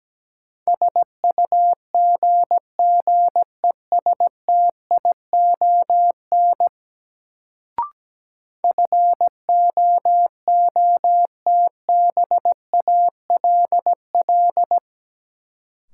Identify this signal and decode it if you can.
Morse code, heard as a keyed tone, about 17 wpm, 700 Hz. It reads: SUGGESTION FOOTBALL
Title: Sentences from Top 1600 Words - New Word Drill Alt - 17wpm